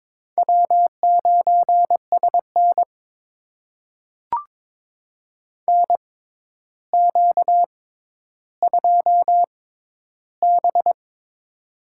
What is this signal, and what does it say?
Morse code, heard as a keyed tone, about 22 wpm, spaced as for 11 wpm, 700 Hz. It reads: W9SN NQ2B
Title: USA Call Signs - 4 Characters - 6x Character Spacing - 22wpm